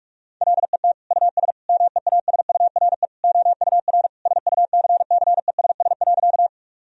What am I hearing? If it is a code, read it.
Morse code, heard as a keyed tone, about 46 wpm, 700 Hz. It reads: LET US MEASURE OUR SUCCESS.
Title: Sentences from Top 1000 Words - Review Part 2 - 45wpm